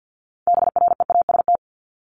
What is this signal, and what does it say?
Morse code, heard as a keyed tone, about 50 wpm, 700 Hz. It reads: BREAST
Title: Top 1751-2000 Words 50wpm